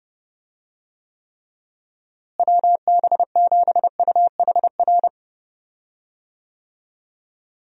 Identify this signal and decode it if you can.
Morse code, heard as a keyed tone, about 32 wpm, 700 Hz. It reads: WB7UHR
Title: USA Call Signs - 6 Characters - Repeated 3x - 30wpm